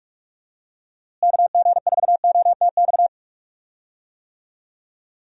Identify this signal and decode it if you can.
Morse code, heard as a keyed tone, about 46 wpm, 700 Hz. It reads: KG4OTX
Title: USA Call Signs - 6 Characters - Repeated 3x - 45wpm